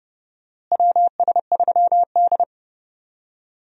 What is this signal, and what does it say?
Morse code, heard as a keyed tone, about 30 wpm, 700 Hz. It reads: WS3D
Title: USA Call Signs - 4 Characters - Repeated 2x - 30wpm